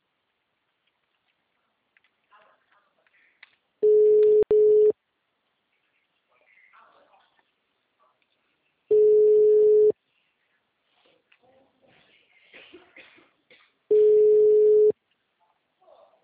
Ringback tone (Europe, 425 Hz).
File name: out- -2015- - - .54119